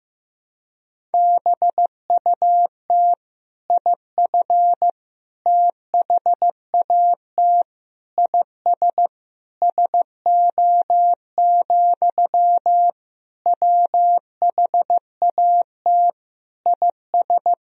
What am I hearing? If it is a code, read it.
Morse code, heard as a keyed tone, about 15 wpm, 700 Hz. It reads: BUT IF THAT IS SO, WHAT IS
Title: Sentences from Top 200 Words and Abbreviations 15wpm